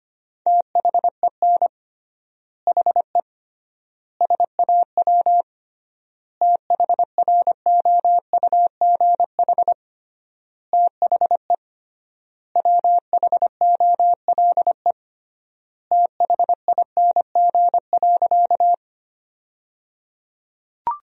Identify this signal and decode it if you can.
Morse code, heard as a keyed tone, about 25 wpm, 700 Hz. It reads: THEN HE SAW THROUGH THE WHOLE THING.
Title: 3x Word-Spacing - Sentences from Top 400 Words - 25wpm